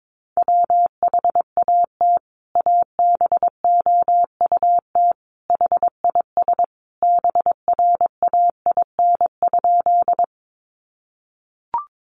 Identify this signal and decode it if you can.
Morse code, heard as a keyed tone, about 22 wpm, 700 Hz. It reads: WHAT ABOUT HIS BRAIN?
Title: Sentences from Top 1100 Words 22wpm